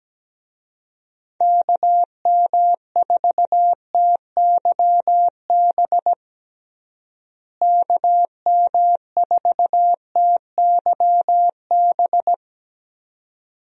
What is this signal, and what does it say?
Morse code, heard as a keyed tone, about 17 wpm, 700 Hz. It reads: KM4TYB KM4TYB
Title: USA Call Signs - 6 Characters - Repeated 2x - 17wpm